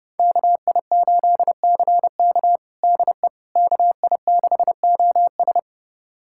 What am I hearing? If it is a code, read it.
Morse code, heard as a keyed tone, about 32 wpm, 700 Hz. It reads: KI8CK DE KI6OS
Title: USA Callsign DE Callsign - 5 Characters - 30wpm